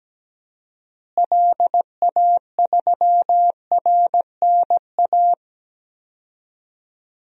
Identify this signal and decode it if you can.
Morse code, heard as a keyed tone, about 17 wpm, 700 Hz. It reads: LA3RNA
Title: DXCC-100 Call Signs - 6 Characters - Repeated 2x - 17wpm